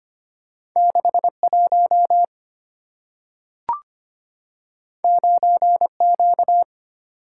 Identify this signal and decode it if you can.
Morse code, heard as a keyed tone, about 25 wpm, 690 Hz. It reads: 61 9Q
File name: Two Letter-Number Combinations - 25wpm